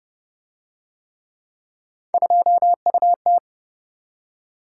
Morse code, heard as a keyed tone, about 30 wpm, 700 Hz. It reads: 2UT